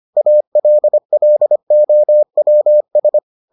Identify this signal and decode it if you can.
Morse code, heard as a keyed tone, about 25 wpm, 600 Hz. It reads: ALLOWS